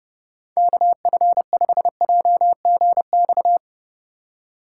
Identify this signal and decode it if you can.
Morse code, heard as a keyed tone, about 30 wpm, 700 Hz. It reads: KF5JGX
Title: USA Call Signs - 6 Characters - 3x Character Spacing - 30wpm